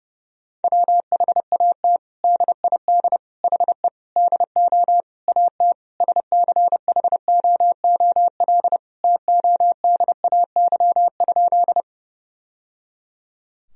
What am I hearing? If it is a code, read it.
Morse code, heard as a keyed tone, about 32 wpm, 700 Hz. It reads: WHAT DID HE DO AT SCHOOL TODAY?
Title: Sentences from Top 1000 Words - Review Part 1 - 30wpm